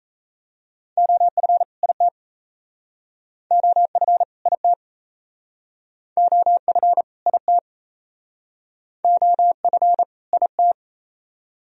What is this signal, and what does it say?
Morse code, heard as a keyed tone, about 36 wpm, 700 Hz. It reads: OF IT OF IT OF IT OF IT